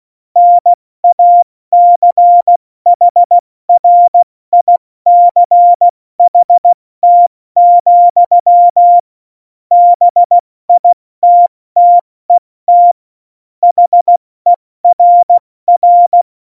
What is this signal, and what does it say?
Morse code, heard as a keyed tone, about 16 wpm, 700 Hz. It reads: NACHRICHT, BITTET HERR
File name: Morsekurs / Witze / Witz 22. September 2020, 70 F80